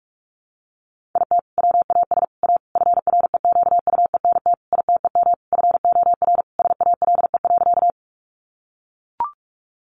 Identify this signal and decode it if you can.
Morse code, heard as a keyed tone, about 44 wpm, 700 Hz. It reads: IT WAS A FREQUENT ITEM FOR SALE.
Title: Sentences from Top 1400 Words 45wpm